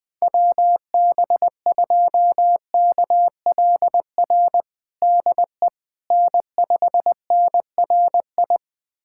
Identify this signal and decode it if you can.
Morse code, heard as a keyed tone, about 21 wpm, 700 Hz. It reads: WB2KLR DE N5NRI
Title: USA Callsign DE Callsign 20wpm